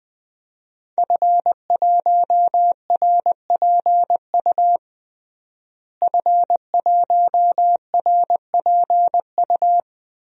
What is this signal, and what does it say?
Morse code, heard as a keyed tone, about 21 wpm, 700 Hz. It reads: F1RPU F1RPU
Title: DXCC-100 Call Signs - 5 Characters - Repeated 2x - 20wpm